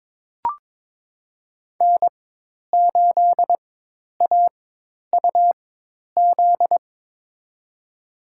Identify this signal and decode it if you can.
Morse code, heard as a keyed tone, about 21 wpm, spaced as for 10 wpm, 700 Hz. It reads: N8AUZ
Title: USA Call Signs - 4x Character Spacing - 22wpm